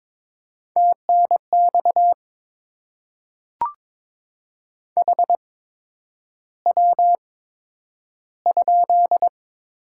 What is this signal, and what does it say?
Morse code, heard as a keyed tone, about 22 wpm, spaced as for 9 wpm, 700 Hz. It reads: TNX HW?
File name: Top 20 QSO Elements - 8x Character Spacing - 22wpm